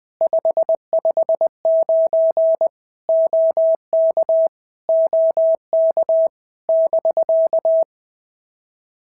Morse code, heard as a keyed tone, about 20 wpm, 650 Hz. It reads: 559 OK OK <BK>
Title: POTA QSO Practice - 20wpm